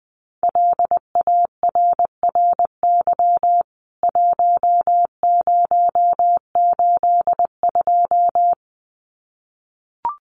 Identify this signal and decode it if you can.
Morse code, heard as a keyed tone, about 20 wpm, 700 Hz. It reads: LARRY 1082
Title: CWT Exchange 20wpm